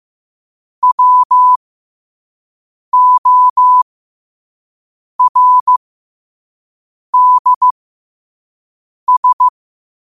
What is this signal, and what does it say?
Morse code, heard as a keyed tone, about 15 wpm, spaced as for 5 wpm, 1000 Hz. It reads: WORDS
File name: qotdfiles / 5wpm